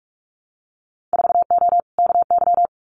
Morse code, heard as a keyed tone, about 46 wpm, 700 Hz. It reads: 4O KY
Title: Winter Field Day Exchanges Encore 45wpm